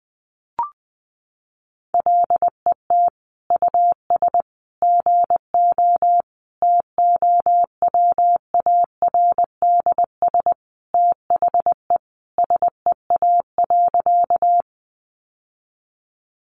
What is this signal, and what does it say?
Morse code, heard as a keyed tone, about 20 wpm, 700 Hz. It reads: LET US GO TOWARDS THE SEA.
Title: Sentences from Top 1100 Words 20wpm